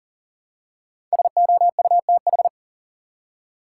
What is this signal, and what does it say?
Morse code, heard as a keyed tone, about 40 wpm, 700 Hz. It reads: SOUTH
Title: Five Letter Words - 40wpm